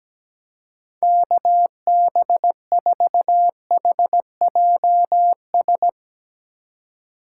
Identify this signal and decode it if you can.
Morse code, heard as a keyed tone, about 17 wpm, 700 Hz. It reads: KB4HJS
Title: Speed Racing - USA 2x3 Call Signs 17wpm